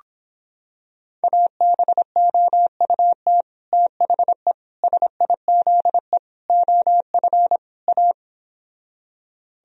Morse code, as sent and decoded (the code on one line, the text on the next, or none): .- -... --- ..- - / - .... / ... .. --.. / --- ..-. / .-
ABOUT THE SIZE OF A